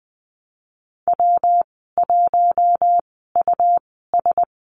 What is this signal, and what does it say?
Morse code, heard as a keyed tone, about 20 wpm, spaced as for 15 wpm, 700 Hz. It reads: W1US
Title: 140 Lesson 23g - Letter f - Call Signs 20wpm